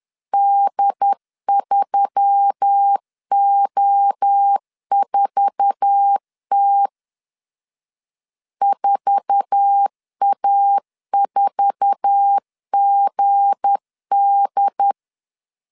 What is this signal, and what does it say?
Morse code, heard as a keyed tone, about 10 wpm, 800 Hz. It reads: D3O4T 4A4GD
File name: LEZIONI / 8 YJ45 / 8 YJ4 10wpm 2633